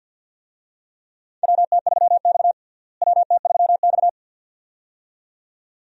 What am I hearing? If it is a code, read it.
Morse code, heard as a keyed tone, about 47 wpm, 700 Hz. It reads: WT3X WT3X
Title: USA Call Signs - 4 Characters - Repeated 2x - 50wpm